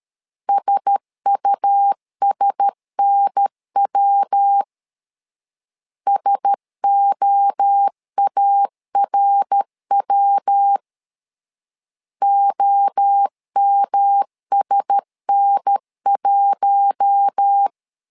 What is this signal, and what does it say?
Morse code, heard as a keyed tone, about 13 wpm, 800 Hz. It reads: SUSNW SOARW OMSN1
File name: LEZIONI / 4 Q1S 12wpm 1802